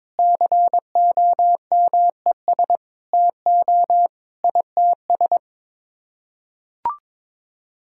Morse code, heard as a keyed tone, about 22 wpm, 700 Hz. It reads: COMES TO ITS
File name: Sets of 9 Words - 22wpm